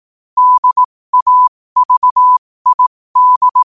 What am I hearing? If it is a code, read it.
Morse code, heard as a keyed tone, about 18 wpm, spaced as for 15 wpm, 1000 Hz. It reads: DAVID
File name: qotdfiles / 15wpm